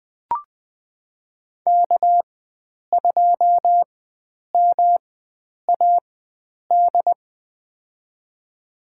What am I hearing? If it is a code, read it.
Morse code, heard as a keyed tone, about 20 wpm, spaced as for 9 wpm, 700 Hz. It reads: K2MAD
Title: USA Call Signs - 5 Characters - 4x Character Spacing - 20wpm